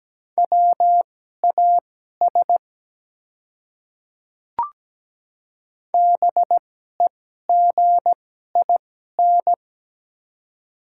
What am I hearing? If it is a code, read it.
Morse code, heard as a keyed tone, about 17 wpm, spaced as for 12 wpm, 690 Hz. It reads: WAS BEGIN
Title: 2x Character Spacing - Top 300 Words - 17wpm